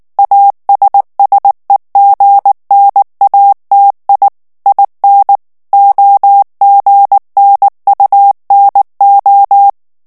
Morse code, heard as a keyed tone, about 19 wpm, 800 Hz. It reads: ASSEGNATI IN OGNUNO
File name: morse mp3 / ITALIEN texte clair 2 vitesse 18 mots minute